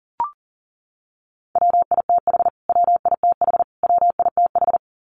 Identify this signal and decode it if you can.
Morse code, heard as a keyed tone, about 40 wpm, 700 Hz. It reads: WITH WITH WITH
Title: Top 300 Words Repeated 40wpm